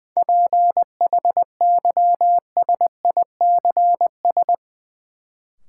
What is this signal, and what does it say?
Morse code, heard as a keyed tone, about 20 wpm, 700 Hz. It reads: PHYSICS
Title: Sentences from Top 1000 Words - Review Part 5 - New Word Drill - 20wpm